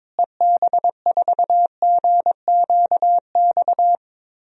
Morse code, heard as a keyed tone, about 22 wpm, 700 Hz. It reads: EB4GQX
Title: DXCC-100 Call Signs - 6 Characters - Repeated 2x - 22wpm